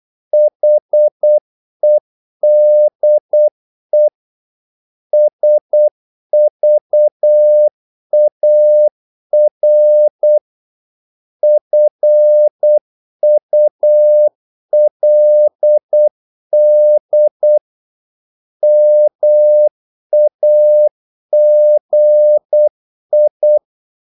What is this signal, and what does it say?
Morse code, heard as a keyed tone, about 8 wpm, 600 Hz. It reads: HEDE SVAR FULD MAGI
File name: Korte ord DK 8wpm